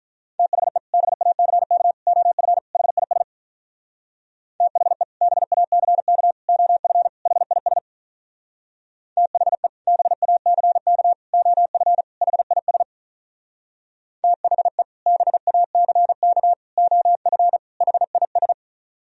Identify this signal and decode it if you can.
Morse code, heard as a keyed tone, about 44 wpm, 700 Hz. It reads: THE BACK OF HIS THE BACK OF HIS THE BACK OF HIS THE BACK OF HIS